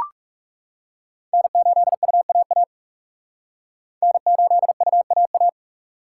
Morse code, heard as a keyed tone, about 36 wpm, 700 Hz. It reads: N8UAA N8UAA
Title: Speed Racing - USA 1x3 Call Signs 30wpm